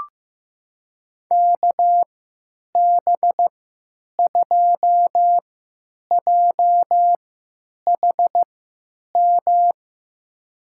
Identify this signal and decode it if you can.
Morse code, heard as a keyed tone, about 15 wpm, spaced as for 9 wpm, 700 Hz. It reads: KB2JHM
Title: USA Call Signs - 6 Characters - 3x Character Spacing - 15wpm